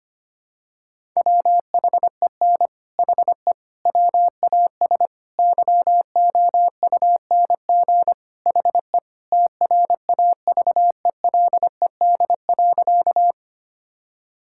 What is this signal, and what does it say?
Morse code, heard as a keyed tone, about 25 wpm, 700 Hz. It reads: WHEN HE WAS YOUNG HE TRAVELED.
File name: practice / 25wpm / Sentences with 3-Word Phrases - 25wpm